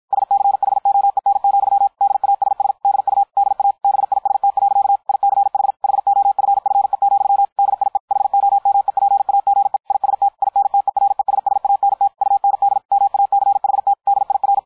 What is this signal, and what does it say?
Morse code, heard as a keyed tone, about 53 wpm, 800 Hz. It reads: SCHOEN, DASS DU DA BIST. ICH HOFFE, DIE HOMEPAGE IST INTERESSANT UND MACHT DIR
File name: dk5ke / Beispiel 250BpM